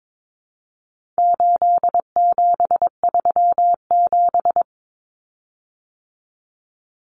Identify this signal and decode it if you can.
Morse code, heard as a keyed tone, about 22 wpm, 700 Hz. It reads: 8737
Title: Common Cut Numbers - 4-Digit 22wpm